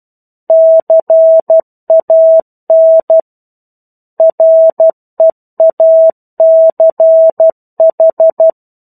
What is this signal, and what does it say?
Morse code, heard as a keyed tone, about 12 wpm, 650 Hz. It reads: CAN REACH
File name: 01-Chapter-01 12 WPM